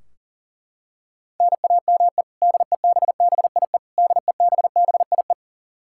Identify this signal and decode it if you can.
Morse code, heard as a keyed tone, about 37 wpm, 700 Hz. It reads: NAME DEBBIE DEBBIE
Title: Practice QSO - Exchange 1 - Beginning - 40wpm